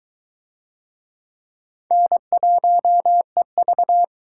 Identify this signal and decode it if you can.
Morse code, heard as a keyed tone, about 20 wpm, 700 Hz. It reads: N1EV